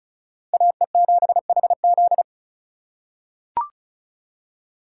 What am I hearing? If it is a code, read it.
Morse code, heard as a keyed tone, about 36 wpm, 700 Hz. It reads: AE7HZ
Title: USA Call Signs - 5 Characters - 3x Character Spacing - 35wpm